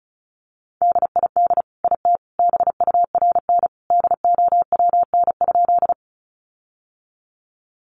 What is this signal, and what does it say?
Morse code, heard as a keyed tone, about 34 wpm, 700 Hz. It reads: DID IT BURN DOWN?
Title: Sentences from Top 1300 Words 35wpm